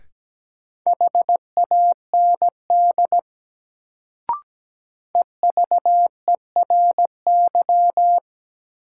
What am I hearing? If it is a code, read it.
Morse code, heard as a keyed tone, about 17 wpm, 700 Hz. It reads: HAND EVERY
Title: Sentences from Top 200 Words - New Word Drill - 17wpm